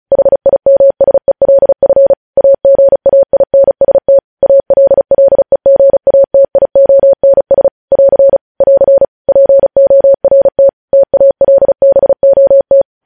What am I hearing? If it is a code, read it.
Morse code, heard as a keyed tone, about 36 wpm, 550 Hz. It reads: HIMSELF AGAINST ALLEGATIONS <AR> <AR> PORT TALBOT
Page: News headlines in Morse code at 35 wpm